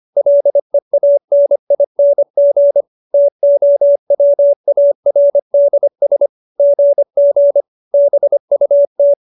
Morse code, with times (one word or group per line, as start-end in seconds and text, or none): 0.2-2.8 LEANING
3.1-6.3 TOWARDS
6.6-7.6 GG
7.9-9.1 BUT